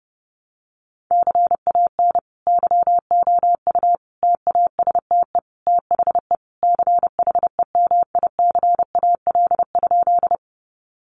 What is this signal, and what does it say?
Morse code, heard as a keyed tone, about 30 wpm, 700 Hz. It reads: CAN YOU TASTE THE CHEMICAL?
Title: Sentences from Top 1200 Words 30wpm